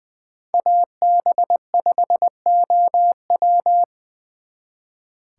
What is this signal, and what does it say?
Morse code, heard as a keyed tone, about 23 wpm, 700 Hz. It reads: AB5OW
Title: Speed Racing - USA 2x2 Call Signs 20wpm